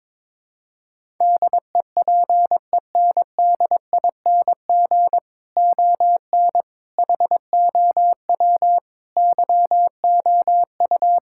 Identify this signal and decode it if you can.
Morse code, heard as a keyed tone, about 22 wpm, 700 Hz. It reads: DEPENDING ON HOW YOU
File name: practice / 22wpm / Sets of 7 Words - 22wpm